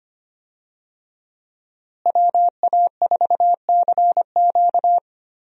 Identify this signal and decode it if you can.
Morse code, heard as a keyed tone, about 25 wpm, 700 Hz. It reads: WA4CQ
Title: USA Call Signs - 5 Characters - Repeated 2x - 25wpm